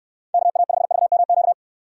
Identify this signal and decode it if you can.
Morse code, heard as a keyed tone, about 69 wpm, 700 Hz. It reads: KA5VMQ